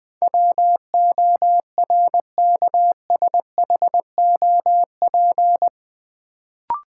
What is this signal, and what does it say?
Morse code, heard as a keyed tone, about 20 wpm, 700 Hz. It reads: WORKSHOP
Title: Build A Word - 4 Letters At A Time - 8-Letter Words - 20wpm